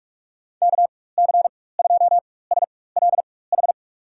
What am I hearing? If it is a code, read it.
Morse code, heard as a keyed tone, about 46 wpm, spaced as for 21 wpm, 700 Hz. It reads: KC2SLH
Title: USA Call Signs - 6 Characters - 4x Character Spacing - 45wpm